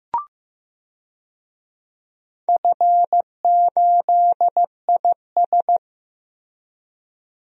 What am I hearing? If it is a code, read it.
Morse code, heard as a keyed tone, about 15 wpm, 700 Hz. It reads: F8IS
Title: DXCC-100 Call Signs - 4 Characters - Repeated 3x - 15wpm